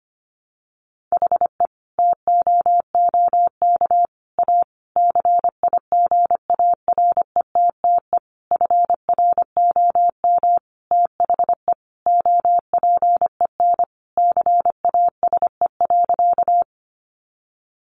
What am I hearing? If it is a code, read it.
Morse code, heard as a keyed tone, about 25 wpm, 700 Hz. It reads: HE TOOK A CIGARETTE FROM THE OPEN CASE.
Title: Sentences from Top 1000 Words 25wpm